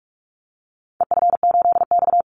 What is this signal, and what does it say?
Morse code, heard as a keyed tone, about 46 wpm, 700 Hz. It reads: EF8X
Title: Call Signs - 45wpm